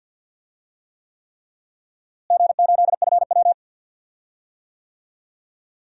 Morse code, heard as a keyed tone, about 50 wpm, 700 Hz. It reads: G8FW
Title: DXCC-100 Call Signs - 4 Characters - Repeated 3x - 50wpm